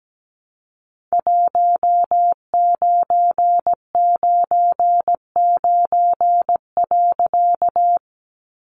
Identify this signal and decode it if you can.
Morse code, heard as a keyed tone, about 17 wpm, 700 Hz. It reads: 1999.
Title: A Sound Of Thunder 17wpm